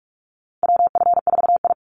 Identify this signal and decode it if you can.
Morse code, heard as a keyed tone, about 44 wpm, 700 Hz. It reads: WF4I
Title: NA SOTA Chaser Callsigns 45wpm